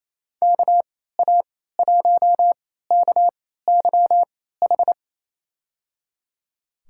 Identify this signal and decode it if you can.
Morse code, heard as a keyed tone, about 28 wpm, spaced as for 16 wpm, 700 Hz. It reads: KA1KYH
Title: USA Call Signs - 6 Characters - 3x Character Spacing - 28wpm